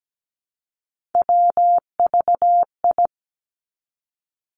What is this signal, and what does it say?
Morse code, heard as a keyed tone, about 17 wpm, 700 Hz. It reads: WVI
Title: Three Challenging Letter-Number Combinations 17wpm